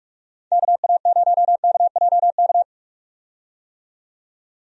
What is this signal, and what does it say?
Morse code, heard as a keyed tone, about 48 wpm, 700 Hz. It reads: KA0KJK
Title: USA Call Signs - 6 Characters - 45wpm